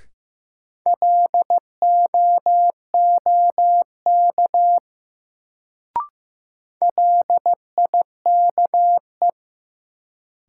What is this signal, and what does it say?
Morse code, heard as a keyed tone, about 15 wpm, 700 Hz. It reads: LOOK LIKE
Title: Sets of 2 Words - New Word Drill - 15wpm